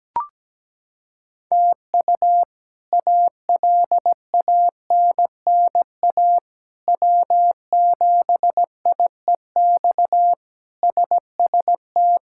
Morse code, heard as a keyed tone, about 17 wpm, 700 Hz. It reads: TU ALANNA W7IEX SST
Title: K1USN SST QSO Practice - 17wpm